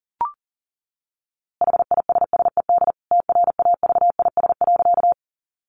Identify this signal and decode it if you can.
Morse code, heard as a keyed tone, about 38 wpm, 700 Hz. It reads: HISSED TRAVIS.
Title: A Sound Of Thunder 40wpm